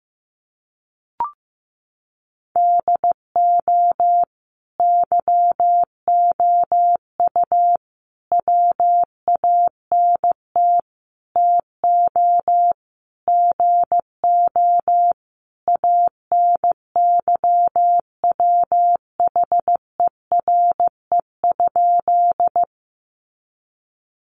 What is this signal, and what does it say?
Morse code, heard as a keyed tone, about 15 wpm, 700 Hz. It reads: DO YOU WANT TO GO ANYWHERE?
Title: Sentences from Top 1400 Words 15wpm